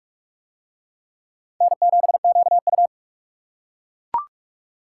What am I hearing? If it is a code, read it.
Morse code, heard as a keyed tone, about 38 wpm, 700 Hz. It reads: N7QU